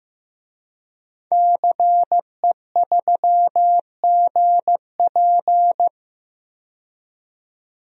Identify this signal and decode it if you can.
Morse code, heard as a keyed tone, about 15 wpm, 700 Hz. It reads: CE3GP
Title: DXCC-100 Call Signs - 5 Characters - 15wpm